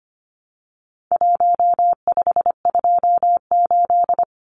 Morse code, heard as a keyed tone, about 25 wpm, 700 Hz. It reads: 1528
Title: Numbers - Contest Style - 4-Digit 25wpm